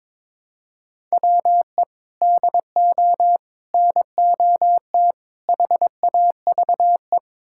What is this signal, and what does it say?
Morse code, heard as a keyed tone, about 21 wpm, 700 Hz. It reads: WE DO NOT HAVE
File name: practice / 22wpm / Build A Phrase - 5-Word Phrases - 22wpm